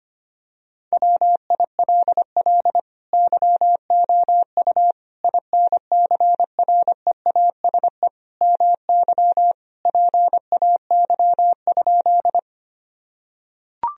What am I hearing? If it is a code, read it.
Morse code, heard as a keyed tone, about 25 wpm, 700 Hz. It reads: WILL YOU INCREASE MY PAY?
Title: Sentences from Top 1000 Words - Review Part 4 - 25wpm